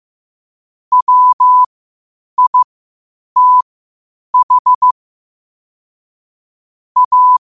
Morse code, heard as a keyed tone, about 15 wpm, spaced as for 8 wpm, 1000 Hz. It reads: WITH A
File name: qotdfiles / 8wpm